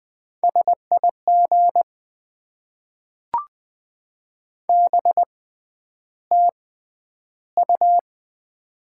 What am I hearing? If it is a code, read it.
Morse code, heard as a keyed tone, about 20 wpm, spaced as for 10 wpm, 700 Hz. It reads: SIG BTU
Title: Top 90 QSO Elements - 6x Character Spacing - 20wpm